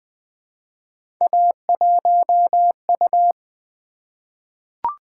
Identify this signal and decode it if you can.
Morse code, heard as a keyed tone, about 20 wpm, 700 Hz. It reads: A1U
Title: Three Letter-Number Combinations - 20wpm